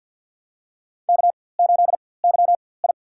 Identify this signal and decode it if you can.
Morse code, heard as a keyed tone, about 53 wpm, spaced as for 24 wpm, 700 Hz. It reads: K8YI
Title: USA Call Signs - 4x Character Spacing - 50wpm